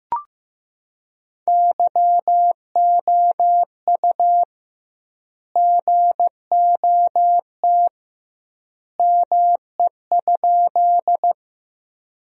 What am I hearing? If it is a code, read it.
Morse code, heard as a keyed tone, about 15 wpm, 700 Hz. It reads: YOU GOT ME?
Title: 2x Word-Spacing - Sentences from Top 300 Words - 15wpm